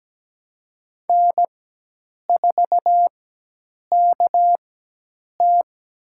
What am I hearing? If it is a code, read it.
Morse code, heard as a keyed tone, about 17 wpm, spaced as for 8 wpm, 700 Hz. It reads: N4KT
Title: USA Call Signs - 4 Characters - 4x Character Spacing - 17wpm